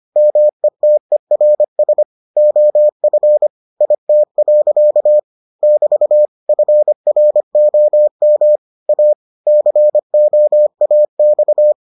Morse code, heard as a keyed tone, about 25 wpm, 600 Hz. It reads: METERS OF IT. <BT> FROM A COAX